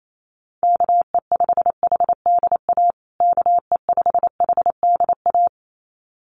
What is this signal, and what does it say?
Morse code, heard as a keyed tone, about 28 wpm, 700 Hz. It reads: KE5HDA KE5HDA
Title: USA Call Signs Repeated 2x 28wpm